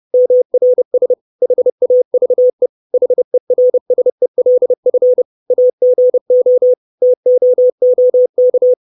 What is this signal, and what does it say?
Morse code, heard as a keyed tone, about 30 wpm, 500 Hz. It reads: MRS HAVE HERSELF AGO TOOK
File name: 632 Words 30WPM